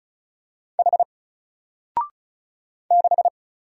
Morse code, heard as a keyed tone, about 34 wpm, 700 Hz. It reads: H 6
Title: Single Letter-Number 35wpm